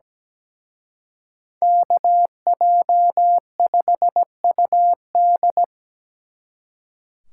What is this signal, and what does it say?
Morse code, heard as a keyed tone, about 20 wpm, 700 Hz. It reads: KJ5UD
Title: Speed Racing - USA 2x2 Call Signs 17wpm